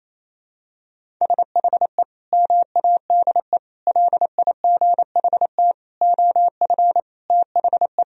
Morse code, heard as a keyed tone, about 28 wpm, 700 Hz. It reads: SHE MADE LIGHT OF THE
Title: Sentences from Top 1000 Words - Review Part 1 - 28wpm